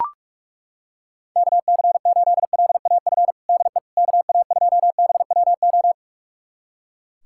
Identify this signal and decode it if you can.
Morse code, heard as a keyed tone, about 41 wpm, 700 Hz. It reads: KC8LAF DE KA2BWO
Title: USA Callsign DE Callsign - 6 Characters - 45wpm